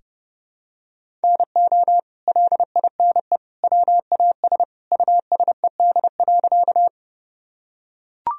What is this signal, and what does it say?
Morse code, heard as a keyed tone, about 32 wpm, 700 Hz. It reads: NO LINE WAS USED.
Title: Sentences from Top 1000 Words - Review Part 1 - 30wpm